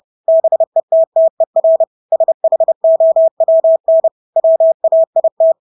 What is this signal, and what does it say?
Morse code, heard as a keyed tone, about 30 wpm, 650 Hz. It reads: BETTER SHOWN WAIT
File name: Engelske ord 30-30 wpm